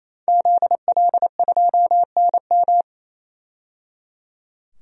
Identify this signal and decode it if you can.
Morse code, heard as a keyed tone, about 28 wpm, 700 Hz. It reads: ZL2NM
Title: DXCC-100 Call Signs - 5 Characters - 28wpm